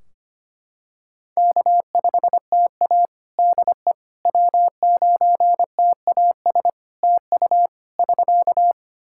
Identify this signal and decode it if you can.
Morse code, heard as a keyed tone, about 25 wpm, 700 Hz. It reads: K5TA DE W9TAS TU <SK>
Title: Practice QSO - Exchange 4 - Ending - 25wpm